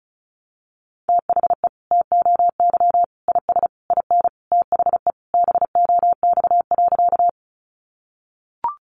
Morse code, heard as a keyed tone, about 34 wpm, 700 Hz. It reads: THE TOY IS IN THE BOX.
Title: Easy Phrases 35wpm